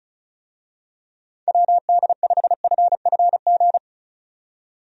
Morse code, heard as a keyed tone, about 36 wpm, 700 Hz. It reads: WD5FFG
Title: USA Call Signs - 6 Characters - 35wpm